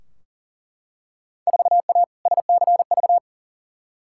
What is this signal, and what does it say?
Morse code, heard as a keyed tone, about 40 wpm, 700 Hz. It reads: 4A SCV
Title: Field Day Exchanges Encore - 40wpm